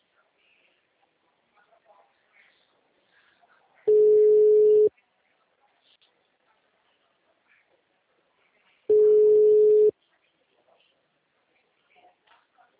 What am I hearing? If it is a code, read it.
Ringback tone (Europe, 425 Hz).